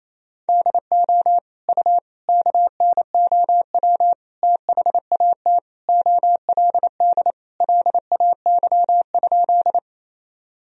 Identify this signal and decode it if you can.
Morse code, heard as a keyed tone, about 28 wpm, 700 Hz. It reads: DO U KNOW THAT OLD LAY?
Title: Sentences from Top 400 Words and Abbreviations 28wpm